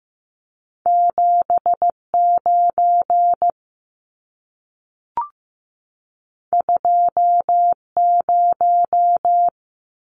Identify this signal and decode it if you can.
Morse code, heard as a keyed tone, about 15 wpm, 700 Hz. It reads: 79 20
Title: Numbers - 2-Digit 15wpm